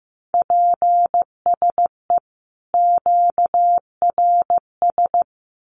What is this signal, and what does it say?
Morse code, heard as a keyed tone, about 15 wpm, 700 Hz. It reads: PSE QRS
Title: Common QSO Phrases 15wpm